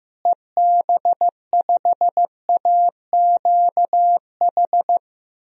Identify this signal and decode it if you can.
Morse code, heard as a keyed tone, about 15 wpm, 700 Hz. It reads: EB5AQH
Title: DXCC-100 Call Signs - 6 Characters - Repeated 3x - 15wpm